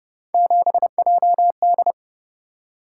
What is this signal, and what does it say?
Morse code, heard as a keyed tone, about 28 wpm, 700 Hz. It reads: 7JD